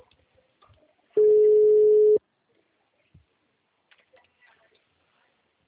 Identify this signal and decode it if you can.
Ringback tone (Europe, 425 Hz), with speech.
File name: out- -2002- - - .193